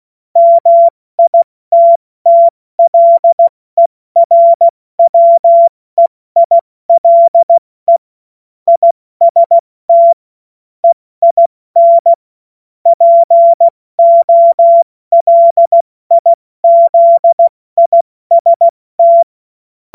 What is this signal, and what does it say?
Morse code, heard as a keyed tone, about 16 wpm, 680 Hz. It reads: MITTLERWEILE IST EIN POLIZIST
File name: Morsekurs / Witze / Witz 17. Mai 2022 70 F80